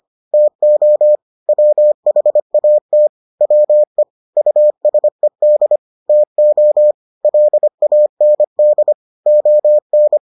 Morse code, heard as a keyed tone, about 25 wpm, 600 Hz. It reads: TO WHAT WE USED TO LAND ON